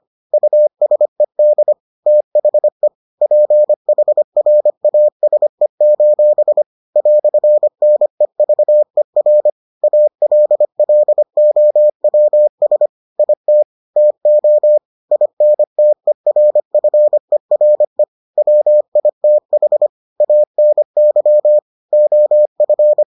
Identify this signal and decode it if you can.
Morse code, heard as a keyed tone, about 25 wpm, 600 Hz. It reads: USED THE PHRASE: "NEVER ALLOWS IT TO INTERFERE WITH ANY OF